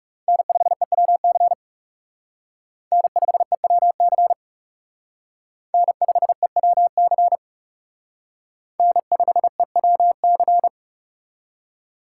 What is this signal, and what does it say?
Morse code, heard as a keyed tone, about 36 wpm, 700 Hz. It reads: N5EWC N5EWC N5EWC N5EWC
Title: Speed Racing - USA 1x3 Call Signs 30wpm